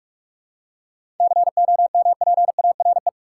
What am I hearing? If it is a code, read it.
Morse code, heard as a keyed tone, about 46 wpm, 700 Hz. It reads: COMPARE
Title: Seven Letter Words - 45wpm